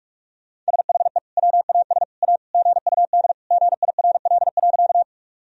Morse code, heard as a keyed tone, about 46 wpm, 700 Hz. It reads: SHE WAS A GUD GIRL.
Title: Sentences from Top 400 Words and Abbreviations 45wpm